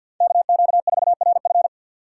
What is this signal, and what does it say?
Morse code, heard as a keyed tone, about 53 wpm, 700 Hz. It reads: KQ4RF
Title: USA Call Signs - 4x Character Spacing - 50wpm